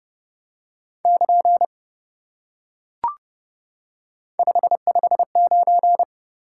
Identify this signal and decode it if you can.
Morse code, heard as a keyed tone, about 30 wpm, 700 Hz. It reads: <KN> 559